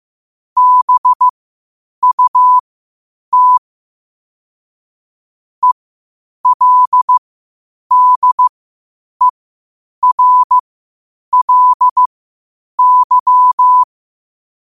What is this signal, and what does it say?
Morse code, heard as a keyed tone, about 15 wpm, spaced as for 8 wpm, 1000 Hz. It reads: BUT ELDERLY
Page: New quotes every day in morse code at 8 Words per minute.